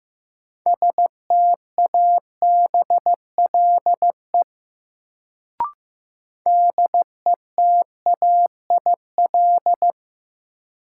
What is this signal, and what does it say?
Morse code, heard as a keyed tone, about 15 wpm, 700 Hz. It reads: STABLE DETAIL
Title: Sentences from Top 1500 Words - New Word Drill Alt - 15wpm